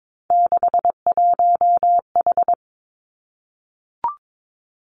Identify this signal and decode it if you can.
Morse code, heard as a keyed tone, about 22 wpm, 700 Hz. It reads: 61H
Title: Three Challenging Letter-Number Combinations 22wpm